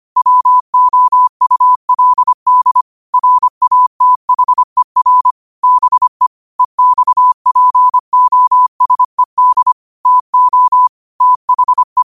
Morse code, heard as a keyed tone, about 25 wpm, 1000 Hz. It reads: WOULD RATHER BE EXPOSED TO THE
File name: qotdfiles / 25wpm